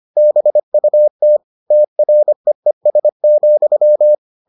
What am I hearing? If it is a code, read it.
Morse code, heard as a keyed tone, about 25 wpm, 600 Hz. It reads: BUT TREES,